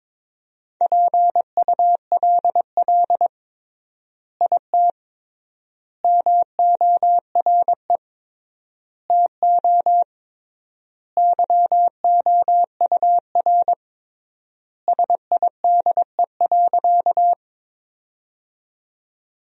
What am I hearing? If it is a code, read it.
Morse code, heard as a keyed tone, about 22 wpm, 700 Hz. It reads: PULL IT MORE TO YOUR SIDE.
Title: 3x Word-Spacing - Sentences from Top 400 Words - 22wpm